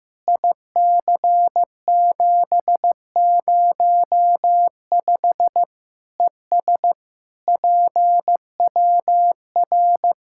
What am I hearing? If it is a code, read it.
Morse code, heard as a keyed tone, about 15 wpm, 700 Hz. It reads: IC705 ES PWR
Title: Practice QSO - Exchange 2 - Info - 15wpm